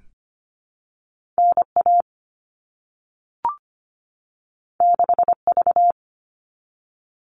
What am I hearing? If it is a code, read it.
Morse code, heard as a keyed tone, about 25 wpm, 700 Hz. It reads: NA 6V
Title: All Cut Numbers - 2-Digit 25wpm